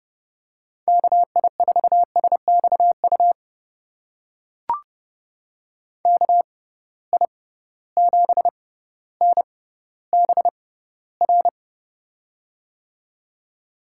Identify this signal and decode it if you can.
Morse code, heard as a keyed tone, about 30 wpm, spaced as for 15 wpm, 700 Hz. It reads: KI4SXU KI7NBR
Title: USA Call Signs - 6 Characters - 6x Character Spacing - 30wpm